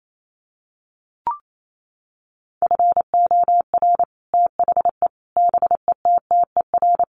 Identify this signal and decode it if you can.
Morse code, heard as a keyed tone, about 28 wpm, 700 Hz. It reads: FOR THE BETTER
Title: Sets of 5 Words Encore2 28wpm